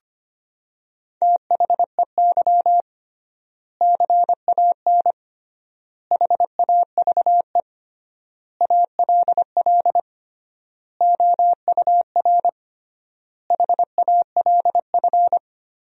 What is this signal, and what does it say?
Morse code, heard as a keyed tone, about 25 wpm, 700 Hz. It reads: THEY CAN HAVE ALL OUR HALF
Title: 3x Word-Spacing - Sentences from Top 400 Words - 25wpm